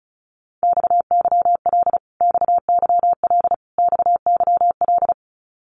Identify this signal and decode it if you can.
Morse code, heard as a keyed tone, about 34 wpm, 700 Hz. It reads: XYL XYL XYL
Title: Abbreviations Repeated 3x - 35wpm